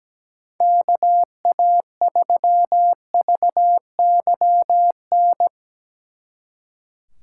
Morse code, heard as a keyed tone, about 19 wpm, 700 Hz. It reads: KA3VYN
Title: Speed Racing - USA Call Signs 17wpm